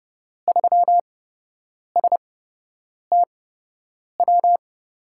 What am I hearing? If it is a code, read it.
Morse code, heard as a keyed tone, about 30 wpm, spaced as for 8 wpm, 700 Hz. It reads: 3STW
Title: USA Call Signs - 5 Characters - 8x Character Spacing - 30wpm